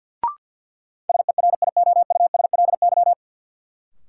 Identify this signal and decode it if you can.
Morse code, heard as a keyed tone, about 47 wpm, 700 Hz. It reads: SERIOUSLY
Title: Sentences from Top 1200 Words - New Word Drill - 50wpm